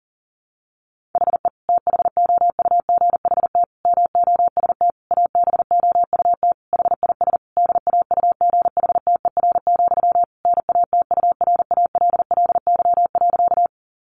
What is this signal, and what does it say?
Morse code, heard as a keyed tone, about 40 wpm, 700 Hz. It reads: HE THOUGHT MOST ABOUT HIS DAUGHTER, NATURALLY.
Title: Sentences from Top 1500 Words 40wpm